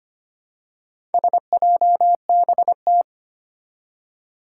Morse code, heard as a keyed tone, about 25 wpm, 700 Hz. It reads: SJBT
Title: Four Challenging Letter-Number Combinations 25wpm